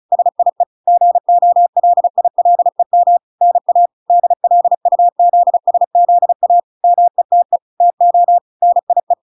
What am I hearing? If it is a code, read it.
Morse code, heard as a keyed tone, about 35 wpm, 700 Hz. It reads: SIE GOLILEM NA DLUZSZA METE TO NIE